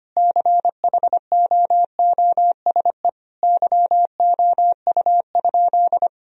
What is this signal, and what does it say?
Morse code, heard as a keyed tone, about 25 wpm, 700 Hz. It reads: CHOOSE YOU?
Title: Sentences from Top 1000 Words - Review Part 4 - 25wpm